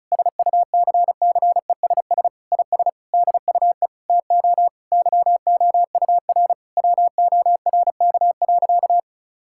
Morse code, heard as a keyed tone, about 36 wpm, 700 Hz. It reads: SUCCESS IS DUE TO YOUR WORK.
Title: Sentences from Top 1000 Words - Review Part 6 - 35wpm